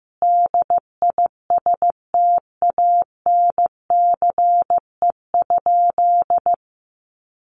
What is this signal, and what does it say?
Morse code, heard as a keyed tone, about 15 wpm, 700 Hz. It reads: DISTANCE?
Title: Sentences from Top 1200 Words 15wpm